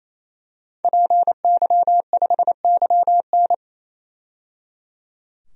Morse code, heard as a keyed tone, about 28 wpm, 700 Hz. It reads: PY5YN
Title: DXCC-100 Call Signs - 5 Characters - 28wpm